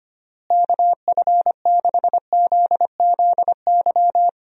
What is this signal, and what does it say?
Morse code, heard as a keyed tone, about 25 wpm, 700 Hz. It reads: KF6ZZY
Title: USA Call Signs - 6 Characters - Repeated 3x - 25wpm